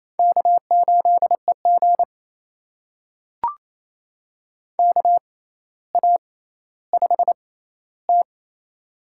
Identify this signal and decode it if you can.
Morse code, heard as a keyed tone, about 28 wpm, spaced as for 14 wpm, 700 Hz. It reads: K8EG KA5T
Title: USA Call Signs - 4 Characters - 6x Character Spacing - 28wpm